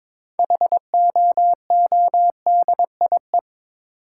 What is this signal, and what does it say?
Morse code, heard as a keyed tone, about 22 wpm, 690 Hz. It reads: HOODIE
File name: Difficult Words with Hints - 22wpm